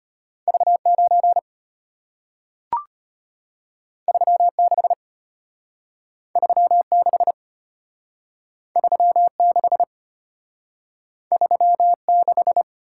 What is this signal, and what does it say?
Morse code, heard as a keyed tone, about 34 wpm, 700 Hz. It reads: V9 36 36 36 36